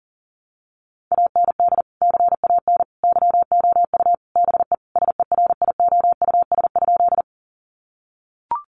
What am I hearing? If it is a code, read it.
Morse code, heard as a keyed tone, about 40 wpm, 700 Hz. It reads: AND CAN YOU BE SERIOUS?
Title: Sentences from Top 1100 Words 40wpm